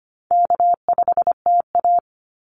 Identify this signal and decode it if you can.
Morse code, heard as a keyed tone, about 25 wpm, 700 Hz. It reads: K5TA
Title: Easy Call Signs - 25wpm